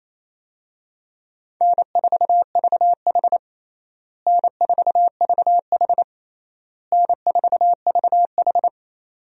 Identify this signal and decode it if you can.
Morse code, heard as a keyed tone, about 28 wpm, 700 Hz. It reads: N4VH N4VH N4VH
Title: USA Call Signs - 4 Characters - Repeated 3x - 28wpm